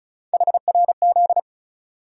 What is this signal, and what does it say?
Morse code, heard as a keyed tone, about 36 wpm, 700 Hz. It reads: HRZ
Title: USA Call Sign Postfixes - 3 Letters 35wpm